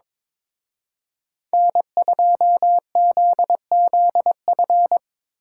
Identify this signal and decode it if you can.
Morse code, heard as a keyed tone, about 25 wpm, 700 Hz. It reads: N2ZZF
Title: Speed Racing - USA 1x3 Call Signs 22wpm